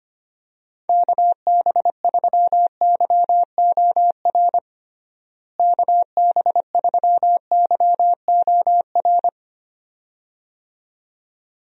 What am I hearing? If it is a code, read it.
Morse code, heard as a keyed tone, about 25 wpm, 700 Hz. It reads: KB3YOR KB3YOR
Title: USA Call Signs - 6 Characters - Repeated 2x - 25wpm